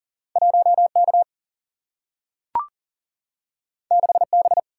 Morse code, heard as a keyed tone, about 39 wpm, 700 Hz. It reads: 1K 6B
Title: Single-Letter Single-Number 40wpm